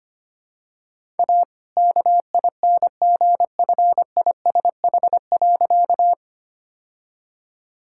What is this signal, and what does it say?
Morse code, heard as a keyed tone, about 25 wpm, 700 Hz. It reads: A KINGFISH.
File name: practice / 25wpm / All-Age Appropriate Jokes - 25wpm